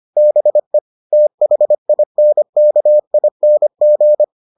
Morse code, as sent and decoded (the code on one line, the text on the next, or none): -... / - .... .. -. -.- .. -. --.
BE THINKING